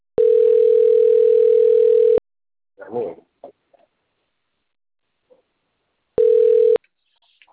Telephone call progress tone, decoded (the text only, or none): ringback tone